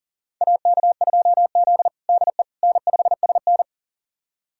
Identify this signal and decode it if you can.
Morse code, heard as a keyed tone, about 37 wpm, 700 Hz. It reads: AK2Z DE N5SN
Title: USA Callsign DE Callsign - 4 Characters - 40wpm